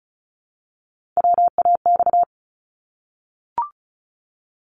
Morse code, heard as a keyed tone, about 36 wpm, 700 Hz. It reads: WAX
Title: Multi-letter Phonemes Repeated 3x 35wpm